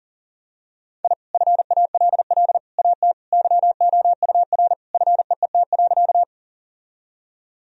Morse code, heard as a keyed tone, about 37 wpm, 700 Hz. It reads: I FALL AT YOUR FEET.
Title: Sentences from Top 1000 Words - Review Part 2 - 40wpm